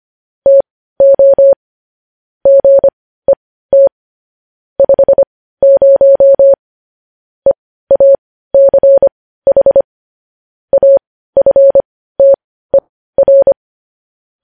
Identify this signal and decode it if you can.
Morse code, heard as a keyed tone, about 25 wpm, spaced as for 15 wpm, 550 Hz. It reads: TO GET 50 EACH AFTER
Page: news headlines in morse code at 15 wpm